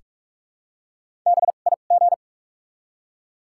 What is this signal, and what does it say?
Morse code, heard as a keyed tone, about 46 wpm, spaced as for 33 wpm, 700 Hz. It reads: BIG
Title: 2x Character Spacing - Top 200 Words - 45wpm